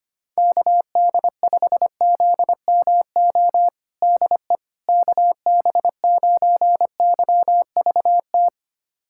Morse code, heard as a keyed tone, about 25 wpm, 700 Hz. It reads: KD5ZMO DE KB9YVT
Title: USA Callsign DE Callsign - 6 Characters - 25wpm